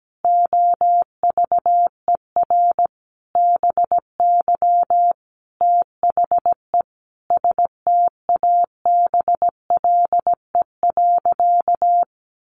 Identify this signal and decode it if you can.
Morse code, heard as a keyed tone, about 17 wpm, 700 Hz. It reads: OVER BY THE STABLE.
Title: Sentences from Top 1500 Words 17wpm